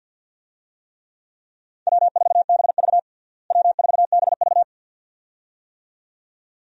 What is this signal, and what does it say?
Morse code, heard as a keyed tone, about 53 wpm, 700 Hz. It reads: W4BV W4BV
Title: USA Call Signs - 4 Characters - Repeated 2x - 50wpm